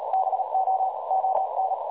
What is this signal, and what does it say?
Morse code, heard as a keyed tone, about 38 wpm, 760 Hz, with noise. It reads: BO ON
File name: dlazaawansowanych / nagraniazpasma / websdr recording 3565.0kHz